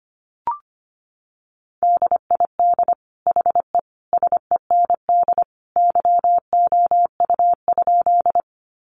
Morse code, heard as a keyed tone, about 25 wpm, 700 Hz. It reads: DID HE SEND YOU?
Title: Sentences from Top 1500 Words 25wpm